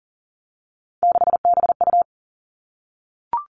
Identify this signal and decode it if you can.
Morse code, heard as a keyed tone, about 42 wpm, 700 Hz. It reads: 6BU
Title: Three Challenging Letter-Number Combinations 40wpm